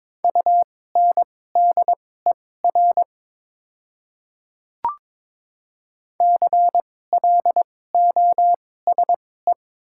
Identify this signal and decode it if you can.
Morse code, heard as a keyed tone, about 22 wpm, spaced as for 16 wpm, 700 Hz. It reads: UNDER CLOSE
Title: 2x Character Spacing - Top 500 Words - 22wpm